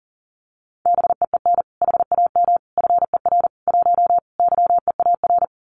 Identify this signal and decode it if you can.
Morse code, heard as a keyed tone, about 40 wpm, 700 Hz. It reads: BEEN HAM FER 1 YEAR
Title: Common QSO Phrases Encore 40wpm